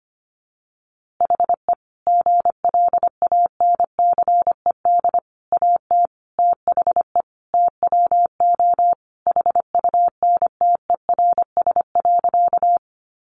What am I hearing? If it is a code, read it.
Morse code, heard as a keyed tone, about 25 wpm, 700 Hz. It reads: HE GLANCED AT THE TWO HUNTERS.
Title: A Sound Of Thunder 25wpm